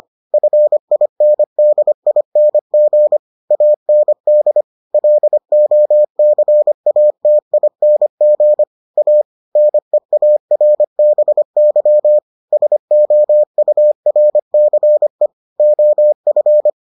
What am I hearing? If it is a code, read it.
Morse code, heard as a keyed tone, about 25 wpm, 600 Hz. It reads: FINDING AND LOCATING A NEARBY SOURCE OF